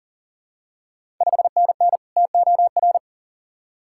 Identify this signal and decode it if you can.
Morse code, heard as a keyed tone, about 40 wpm, 690 Hz. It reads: 5NN TOR
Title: Ontario QSO Party Exchange 40wpm